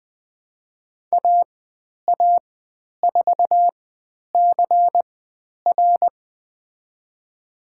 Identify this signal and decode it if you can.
Morse code, heard as a keyed tone, about 20 wpm, spaced as for 10 wpm, 700 Hz. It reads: AA4CR
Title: 095 Lesson 16g - Letter c - Call Signs 20wpm